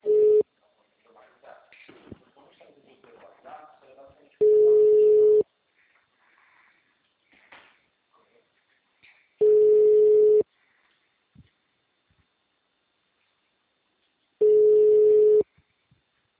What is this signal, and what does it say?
Ringback tone (Europe, 425 Hz), with speech.
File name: out- -2010- - - .69190